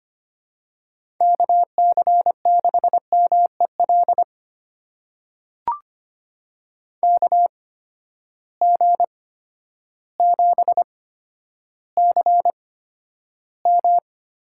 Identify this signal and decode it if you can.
Morse code, heard as a keyed tone, about 25 wpm, spaced as for 11 wpm, 700 Hz. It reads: KC6MEL KG7CM
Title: USA Call Signs - 6 Characters - 8x Character Spacing - 25wpm